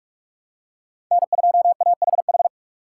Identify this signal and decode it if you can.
Morse code, heard as a keyed tone, about 45 wpm, 700 Hz. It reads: N2AHH
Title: USA Call Signs - 5 Characters - 3x Character Spacing - 45wpm